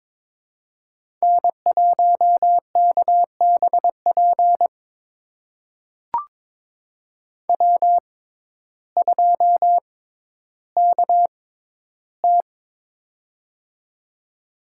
Morse code, heard as a keyed tone, about 22 wpm, spaced as for 12 wpm, 700 Hz. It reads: N1KBP W2KT
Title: USA Call Signs - 6x Character Spacing - 22wpm